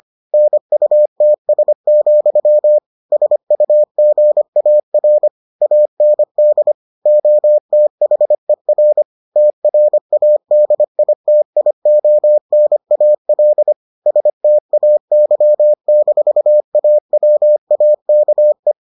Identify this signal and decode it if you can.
Morse code, heard as a keyed tone, about 25 wpm, 600 Hz. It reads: NUTS, SUGAR AND OTHER TRADITIONAL STAY-AWAKE